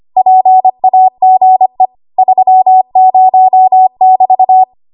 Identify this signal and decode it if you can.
Morse code, heard as a keyed tone, about 25 wpm, 750 Hz. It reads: PAGE 30<BT>
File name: Archive / 25 WPM / 25WPM